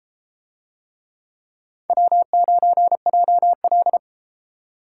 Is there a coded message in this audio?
Yes: Morse code from 1.9 to 4.0 s, heard as a keyed tone, about 33 wpm, 700 Hz. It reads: W9JL